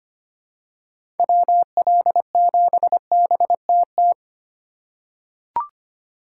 Morse code, heard as a keyed tone, about 25 wpm, 700 Hz. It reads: WL7BTT
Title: USA Call Signs - 6 Characters - 3x Character Spacing - 25wpm